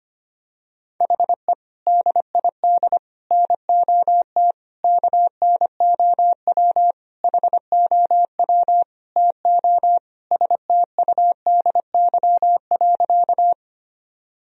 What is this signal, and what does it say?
Morse code, heard as a keyed tone, about 25 wpm, 700 Hz. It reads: HE DID NOT KNOW HOW TO STUDY.
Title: Sentences from Top 1000 Words - Review Part 1 - 25wpm